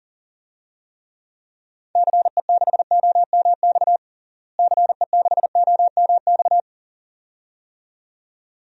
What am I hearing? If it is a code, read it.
Morse code, heard as a keyed tone, about 40 wpm, 700 Hz. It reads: CE6OMX CE6OMX
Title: DXCC-100 Call Signs - 6 Characters - Repeated 2x - 40wpm